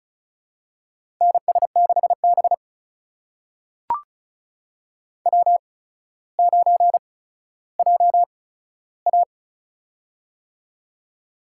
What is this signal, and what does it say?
Morse code, heard as a keyed tone, about 36 wpm, spaced as for 15 wpm, 700 Hz. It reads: NS6B W9JA
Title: USA Call Signs - 4 Characters - 8x Character Spacing - 35wpm